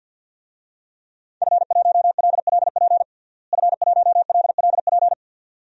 Morse code, heard as a keyed tone, about 51 wpm, 700 Hz. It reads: F1LLP F1LLP
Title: DXCC-100 Call Signs - 5 Characters - Repeated 2x - 50wpm